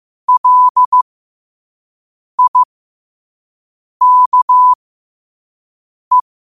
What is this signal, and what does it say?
Morse code, heard as a keyed tone, about 15 wpm, spaced as for 5 wpm, 1000 Hz. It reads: LIKE
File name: 5wpm